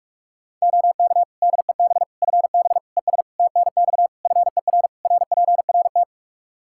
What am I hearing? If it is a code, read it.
Morse code, heard as a keyed tone, about 46 wpm, 700 Hz. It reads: OK DEB FB ES TNX FER RPRT
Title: Practice QSO - Exchange 2 - Info - 45wpm